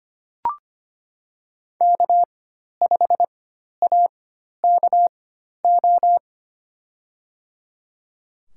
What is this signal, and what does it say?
Morse code, heard as a keyed tone, about 25 wpm, spaced as for 12 wpm, 700 Hz. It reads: K5AKO
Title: USA Call Signs - 5 Characters - 4x Character Spacing - 25wpm